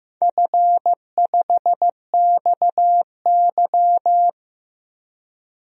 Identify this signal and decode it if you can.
Morse code, heard as a keyed tone, about 15 wpm, 700 Hz. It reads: F5XY
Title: DXCC-100 Call Signs - 4 Characters - 15wpm